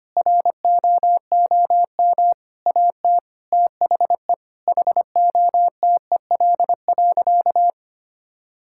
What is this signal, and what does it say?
Morse code, heard as a keyed tone, about 25 wpm, 700 Hz. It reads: ROOM AT THE HOTEL.
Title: Sentences from Top 1600 Words 25wpm